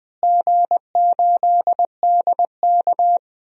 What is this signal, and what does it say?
Morse code, heard as a keyed tone, about 20 wpm, 700 Hz. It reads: G8DK
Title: DXCC-100 Call Signs - 4 Characters - Repeated 3x - 20wpm